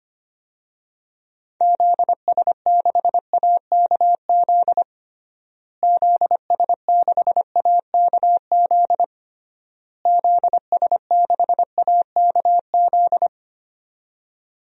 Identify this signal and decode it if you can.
Morse code, heard as a keyed tone, about 25 wpm, 700 Hz. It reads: ZS6AKZ ZS6AKZ ZS6AKZ
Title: DXCC-100 Call Signs - 6 Characters - Repeated 3x - 25wpm